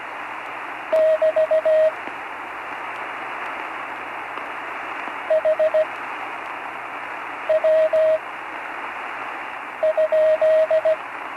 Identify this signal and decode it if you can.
Morse code, heard as a keyed tone, about 17 wpm, spaced as for 5 wpm, 630 Hz, with radio noise. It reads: <BT> HW?
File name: CW Learning Programs / Learning Morse Code (from Tune in the World) / 35 answers